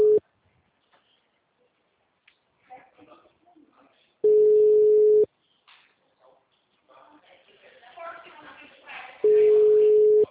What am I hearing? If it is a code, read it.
Ringback tone (Europe, 425 Hz), with speech.
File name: out- -2004- - - .64717